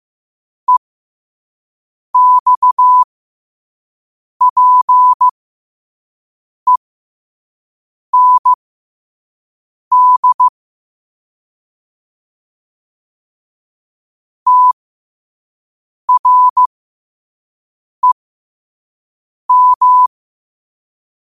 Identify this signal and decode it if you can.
Morse code, heard as a keyed tone, about 15 wpm, spaced as for 5 wpm, 1000 Hz. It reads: EXPEND TREM
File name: qotdfiles / 5wpm